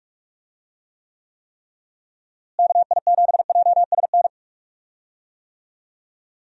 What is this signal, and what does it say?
Morse code, heard as a keyed tone, about 41 wpm, 700 Hz. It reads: KI7JSN
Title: USA Call Signs - 6 Characters - Repeated 3x - 45wpm